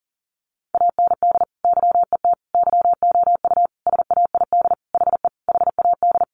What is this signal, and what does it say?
Morse code, heard as a keyed tone, about 40 wpm, 700 Hz. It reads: AND YET YOU SAID HE HAD
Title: Sentences from Top 500 Words 40wpm